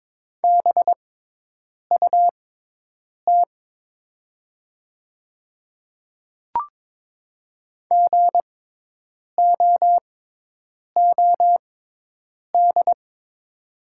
Morse code, heard as a keyed tone, about 22 wpm, spaced as for 8 wpm, 700 Hz. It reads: BUT GOOD
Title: 6x Character Spacing - Top 300 Words - 22wpm